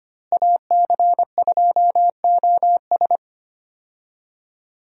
Morse code, heard as a keyed tone, about 25 wpm, 700 Hz. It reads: AC2OS
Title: USA Call Signs - 5 Characters - Repeated 2x - 25wpm